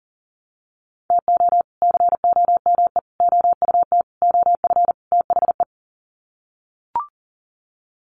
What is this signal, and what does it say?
Morse code, heard as a keyed tone, about 40 wpm, 700 Hz. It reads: TO COME OUT OF THE
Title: Sets of 5 Words Encore2 40wpm